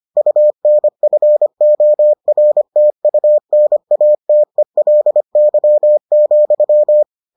Morse code, heard as a keyed tone, about 25 wpm, 600 Hz. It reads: UNFORTUNATELY,